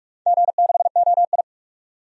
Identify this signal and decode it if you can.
Morse code, heard as a keyed tone, about 42 wpm, 700 Hz. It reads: G6OI
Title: DXCC-100 Call Signs - 4 Characters - 45wpm